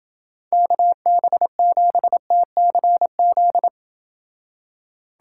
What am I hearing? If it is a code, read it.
Morse code, heard as a keyed tone, about 29 wpm, 700 Hz. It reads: KB7TCZ